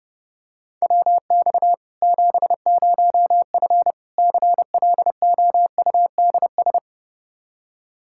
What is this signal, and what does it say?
Morse code, heard as a keyed tone, about 30 wpm, 700 Hz. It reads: WX 70F CLOUDS
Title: K1USN SST QSO Practice 30wpm